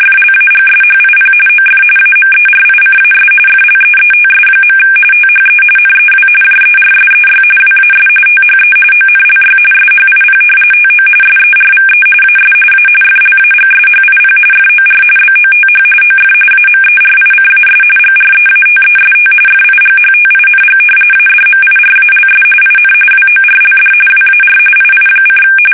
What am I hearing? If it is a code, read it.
RTTY, about 45.45 baud. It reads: FLAGSHIP OF AMERICAS FLEETS. THE NEW JERSEY WAS THE ONLY SHIP TO SERVE AS A FLAGSHIP FOR BOTH OF AMERICAS GREATEST FIGHTING ADMIRALS IN WORLD WAR II:
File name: fsk850 / battleshipnj